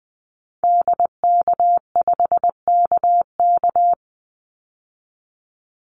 Morse code, heard as a keyed tone, about 19 wpm, 700 Hz. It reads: DK5KK
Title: Call Signs Repeated 2x - 20wpm